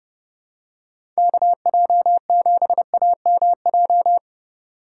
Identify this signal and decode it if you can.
Morse code, heard as a keyed tone, about 30 wpm, 700 Hz. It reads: KJ7AMJ
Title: USA Call Signs - 6 Characters - 3x Character Spacing - 30wpm